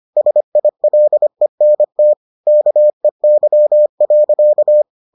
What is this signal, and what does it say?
Morse code, heard as a keyed tone, about 25 wpm, 600 Hz. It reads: SILENT KEY.